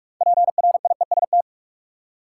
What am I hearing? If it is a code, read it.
Morse code, heard as a keyed tone, about 46 wpm, 700 Hz. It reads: PRIEST
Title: Six Letter Words - 45wpm